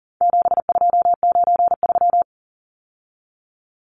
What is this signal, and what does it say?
Morse code, heard as a keyed tone, about 40 wpm, 700 Hz. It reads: 7293
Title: Numbers - 4-Digit 40wpm